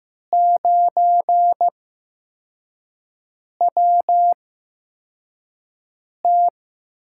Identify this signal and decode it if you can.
Morse code, heard as a keyed tone, about 15 wpm, spaced as for 4 wpm, 700 Hz. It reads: 9WT
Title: USA Call Signs - 8x Character Spacing - 15wpm